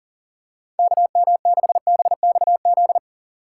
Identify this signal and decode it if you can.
Morse code, heard as a keyed tone, about 40 wpm, 700 Hz. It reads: KM6BXZ
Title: USA Call Signs - 6 Characters - 6x Character Spacing - 40wpm